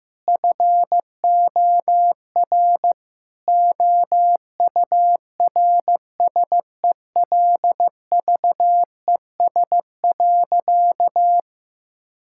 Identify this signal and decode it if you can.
Morse code, heard as a keyed tone, about 15 wpm, 700 Hz. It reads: FOR OURSELVES.
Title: Sentences from Top 1600 Words 15wpm